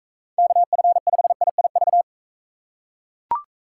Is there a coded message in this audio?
Yes: Morse code from 0.4 to 2.0 s, heard as a keyed tone, about 42 wpm, 700 Hz. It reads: KF5IIV